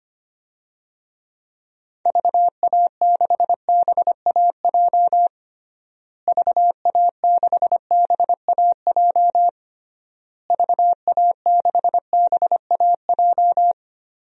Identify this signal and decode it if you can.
Morse code, heard as a keyed tone, about 25 wpm, 700 Hz. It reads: VA6BAJ VA6BAJ VA6BAJ
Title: DXCC-100 Call Signs - 6 Characters - Repeated 3x - 25wpm